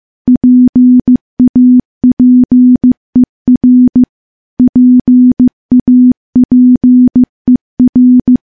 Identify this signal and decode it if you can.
Morse code, heard as a keyed tone, about 15 wpm, 250 Hz. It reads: PAPER PAPER
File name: morse / morse-training-15WPM